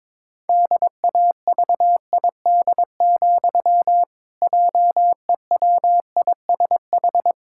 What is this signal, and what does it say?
Morse code, heard as a keyed tone, about 22 wpm, 700 Hz. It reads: DAVID, JEWISH
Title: Pangram Challenge v2 - 22wpm